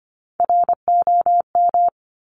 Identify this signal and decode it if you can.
Morse code, heard as a keyed tone, about 25 wpm, 700 Hz. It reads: ROM
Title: Top Three Letter Combinations in Words 25wpm